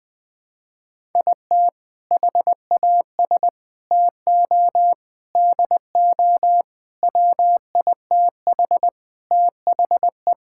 Morse code, heard as a keyed tone, about 20 wpm, 700 Hz. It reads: IT HAS TO DO WITH THE
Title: Sentences with 5-Word Phrases - 20wpm